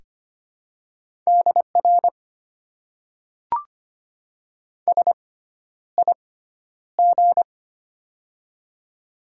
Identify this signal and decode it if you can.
Morse code, heard as a keyed tone, about 25 wpm, spaced as for 11 wpm, 690 Hz. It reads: DR SIG
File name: Top 90 QSO Elements - 6x Character Spacing - 25wpm